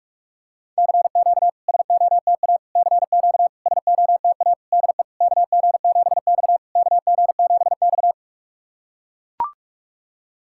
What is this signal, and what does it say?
Morse code, heard as a keyed tone, about 43 wpm, 690 Hz. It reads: CQ SOTA CQ SOTA DE KG7X KG7X
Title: SOTA QSO Practice 45wpm